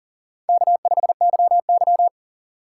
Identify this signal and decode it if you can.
Morse code, heard as a keyed tone, about 38 wpm, 700 Hz. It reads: K5YY
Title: USA Call Signs - 4x Character Spacing - 40wpm